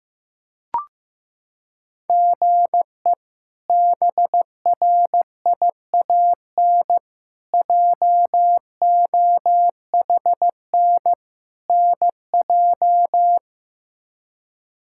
Morse code, heard as a keyed tone, about 15 wpm, 700 Hz. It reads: GE BRIAN JOHN NJ
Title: K1USN SST QSO Practice - 15wpm